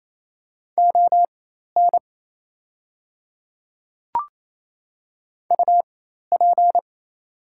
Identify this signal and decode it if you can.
Morse code, heard as a keyed tone, about 28 wpm, spaced as for 13 wpm, 690 Hz. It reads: ON UP
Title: 4x Character Spacing - Top 100 Words - 28wpm